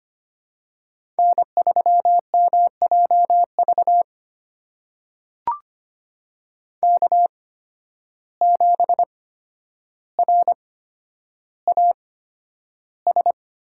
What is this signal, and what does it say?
Morse code, heard as a keyed tone, about 25 wpm, spaced as for 11 wpm, 700 Hz. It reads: N3MJV K7RAS
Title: USA Call Signs - 5 Characters - 8x Character Spacing - 25wpm